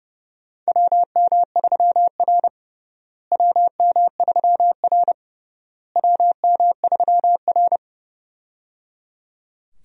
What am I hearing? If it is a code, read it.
Morse code, heard as a keyed tone, about 29 wpm, 700 Hz. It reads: WM3R WM3R WM3R
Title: USA Call Signs - 4 Characters - Repeated 3x - 30wpm